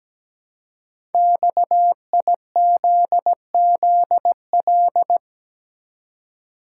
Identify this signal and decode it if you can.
Morse code, heard as a keyed tone, about 17 wpm, 700 Hz. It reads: XIZZL
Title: Code Groups - 5 Letters - 17wpm